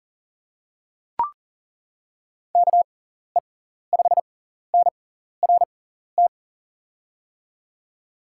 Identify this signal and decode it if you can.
Morse code, heard as a keyed tone, about 40 wpm, spaced as for 14 wpm, 700 Hz. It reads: KE5NRT
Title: USA Call Signs - 6 Characters - 6x Character Spacing - 40wpm